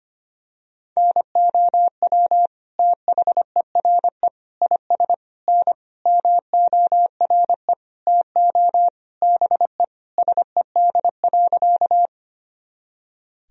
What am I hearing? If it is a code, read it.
Morse code, heard as a keyed tone, about 25 wpm, 700 Hz. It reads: NOW THERE IS N MORE TO BE SED.
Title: Sentences from Top 100 Words and Abbreviations 25wpm